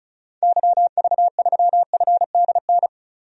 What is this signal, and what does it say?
Morse code, heard as a keyed tone, about 36 wpm, 700 Hz. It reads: YV3FDN
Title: DXCC-100 Call Signs - 6 Characters - Repeated 3x - 35wpm